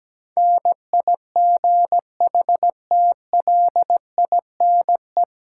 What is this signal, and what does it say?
Morse code, heard as a keyed tone, about 17 wpm, 700 Hz. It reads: NIGHTLINE
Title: Sets of 5 Words - New Word Drill - 17wpm